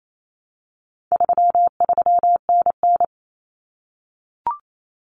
Morse code, heard as a keyed tone, about 28 wpm, 700 Hz. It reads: 33NN
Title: Common Cut Numbers - 4-Digit 28wpm